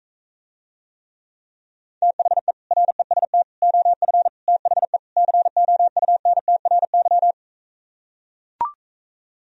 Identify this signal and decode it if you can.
Morse code, heard as a keyed tone, about 44 wpm, 700 Hz. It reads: THE REST OF THE COUNTRY